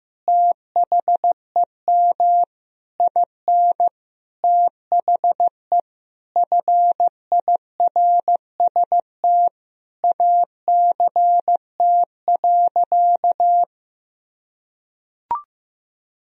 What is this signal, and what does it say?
Morse code, heard as a keyed tone, about 16 wpm, 700 Hz. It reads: THEM IN THE FIRST ACT.
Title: Sentences from Top 1000 Words - Review Part 2 - 15wpm